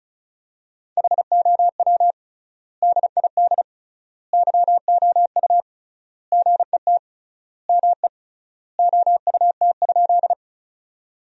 Morse code, heard as a keyed tone, about 35 wpm, 700 Hz. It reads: HOW DID YOU GET ME OUT?
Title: 3x Word-Spacing - Sentences from Top 200 Words - 35wpm